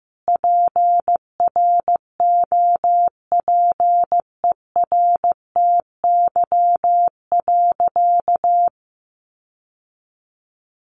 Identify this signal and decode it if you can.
Morse code, heard as a keyed tone, about 15 wpm, 700 Hz. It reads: PROPERTY.
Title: Sentences from Top 1000 Words 15wpm